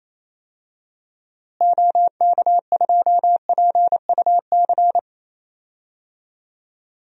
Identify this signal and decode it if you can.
Morse code, heard as a keyed tone, about 28 wpm, 700 Hz. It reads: OK2PUC
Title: DXCC-100 Call Signs - 6 Characters - Repeated 3x - 28wpm